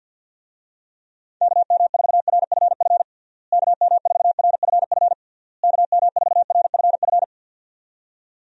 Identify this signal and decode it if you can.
Morse code, heard as a keyed tone, about 51 wpm, 700 Hz. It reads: KM4RFF KM4RFF KM4RFF
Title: USA Call Signs - 6 Characters - Repeated 3x - 50wpm